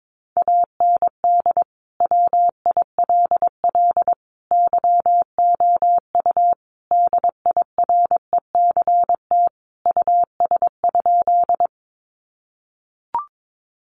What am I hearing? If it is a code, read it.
Morse code, heard as a keyed tone, about 22 wpm, 700 Hz. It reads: AND WILL YOU DIRECT US?
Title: Sentences from Top 400 Words 22wpm